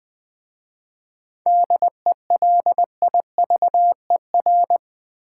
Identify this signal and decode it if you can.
Morse code, heard as a keyed tone, about 20 wpm, 700 Hz. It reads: DELIVER
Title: 7 Letter Words - First 3 Letters Given - 20wpm